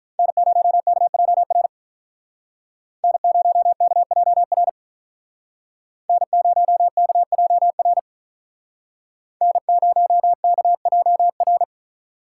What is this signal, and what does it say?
Morse code, heard as a keyed tone, about 43 wpm, 700 Hz. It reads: N0KJR N0KJR N0KJR N0KJR
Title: Speed Racing - USA 1x3 Call Signs 35wpm